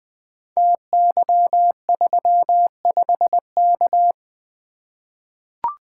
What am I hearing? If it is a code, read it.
Morse code, heard as a keyed tone, about 20 wpm, 700 Hz. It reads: TY35K
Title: Code Groups - 5 Letter:Numbers - 20wpm